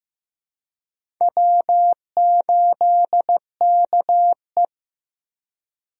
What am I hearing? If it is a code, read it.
Morse code, heard as a keyed tone, about 15 wpm, 700 Hz. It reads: W8KE
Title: USA Call Signs - 4 Characters - Repeated 3x - 15wpm